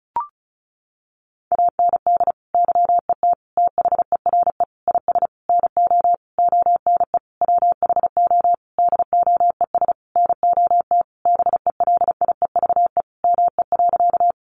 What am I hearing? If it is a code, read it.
Morse code, heard as a keyed tone, about 34 wpm, 700 Hz. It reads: AND YET THERE IS NO ONE WHO DOES NOT BELIEVE ME.
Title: Sentences from Top 1000 Words 35wpm